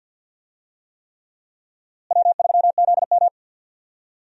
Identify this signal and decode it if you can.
Morse code, heard as a keyed tone, about 52 wpm, 700 Hz. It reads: W3ZM
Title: USA Call Signs - 4 Characters - Repeated 3x - 50wpm